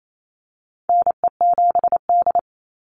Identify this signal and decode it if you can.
Morse code, heard as a keyed tone, about 28 wpm, 700 Hz. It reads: NE7D
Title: Call Signs - 28wpm